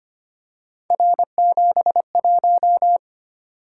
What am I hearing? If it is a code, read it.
Morse code, heard as a keyed tone, about 25 wpm, 700 Hz. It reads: R71
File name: Three Letter-Number Combinations - 25wpm